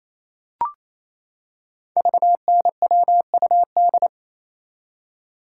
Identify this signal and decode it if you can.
Morse code, heard as a keyed tone, about 28 wpm, 700 Hz. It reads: VNWUD
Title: Code Groups - 5 Letters - 28wpm